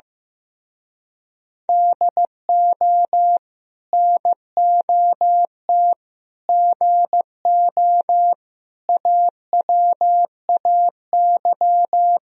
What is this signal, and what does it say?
Morse code, heard as a keyed tone, about 16 wpm, 700 Hz. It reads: DO NOT GO AWAY
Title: Speed-Racing - Sets of 4 Words 15wpm